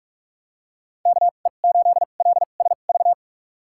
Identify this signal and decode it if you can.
Morse code, heard as a keyed tone, about 44 wpm, spaced as for 32 wpm, 700 Hz. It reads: KE8LSV
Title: USA Call Signs - 6 Characters - 2x Character Spacing - 45wpm